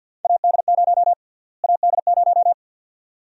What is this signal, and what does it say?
Morse code, heard as a keyed tone, about 53 wpm, 700 Hz. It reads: AD0 AD0
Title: USA Call Sign Prefixes - Repeated 2x - 50wpm